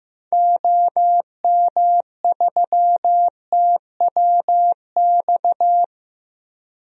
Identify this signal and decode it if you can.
Morse code, heard as a keyed tone, about 15 wpm, 700 Hz. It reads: OM3TWX
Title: DXCC-100 Call Signs - 6 Characters - 15wpm